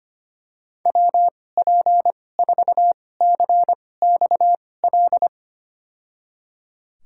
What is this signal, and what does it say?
Morse code, heard as a keyed tone, about 25 wpm, spaced as for 18 wpm, 700 Hz. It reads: WP4CXL
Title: USA Call Signs - 6 Characters - 2x Character Spacing - 25wpm